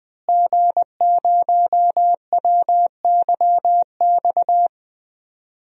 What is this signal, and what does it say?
Morse code, heard as a keyed tone, about 20 wpm, 700 Hz. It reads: G0WYX
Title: DXCC-100 Call Signs - 5 Characters - Repeated 2x - 20wpm